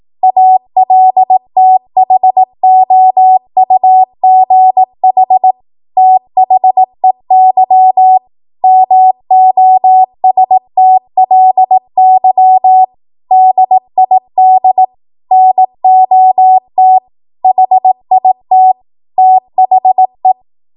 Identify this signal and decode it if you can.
Morse code, heard as a keyed tone, about 18 wpm, 750 Hz. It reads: ALTHOUGH THEY MOSTLY DID NOT HIT THE
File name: Morse / 18WPM